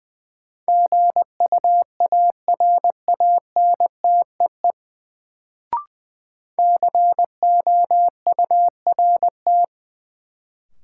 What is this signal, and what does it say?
Morse code, heard as a keyed tone, about 21 wpm, 700 Hz. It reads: GUARANTEE COURT
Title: Sentences from Top 1600 Words - New Word Drill - 20wpm